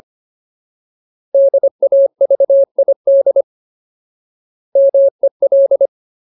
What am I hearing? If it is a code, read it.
Morse code, heard as a keyed tone, about 25 wpm, 550 Hz. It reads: DAVID MEL
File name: Ultimate Word List / 25WPM Ultimate